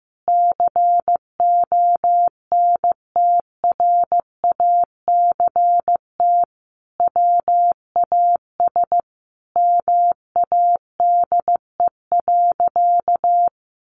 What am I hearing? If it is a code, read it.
Morse code, heard as a keyed tone, about 15 wpm, 700 Hz. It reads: CONTRACT WAS MADE.
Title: Sentences from Top 1400 Words 15wpm